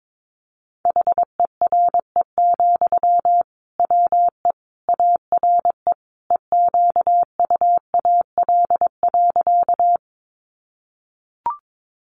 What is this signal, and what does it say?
Morse code, heard as a keyed tone, about 22 wpm, 700 Hz. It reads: HERE, WE ARE EQUAL.
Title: Sentences from Top 1400 Words 22wpm